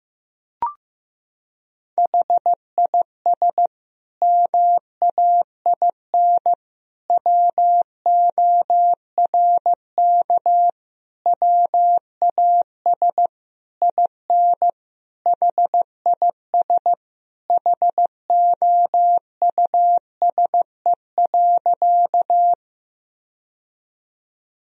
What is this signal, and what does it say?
Morse code, heard as a keyed tone, about 15 wpm, 700 Hz. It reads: HIS MAIN WORK WAS IN HIS HOUSE.
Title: Sentences from Top 1000 Words - Review Part 2 - 15wpm